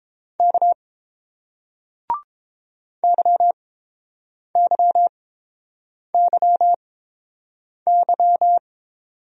Morse code, heard as a keyed tone, about 30 wpm, spaced as for 7 wpm, 700 Hz. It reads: K YYYY